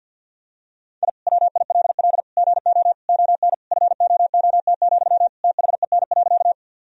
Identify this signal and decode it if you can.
Morse code, heard as a keyed tone, about 49 wpm, 700 Hz. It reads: I WILL GO ON FOOT, THEN.
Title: Sentences from Top 1000 Words - Review Part 3 - 50wpm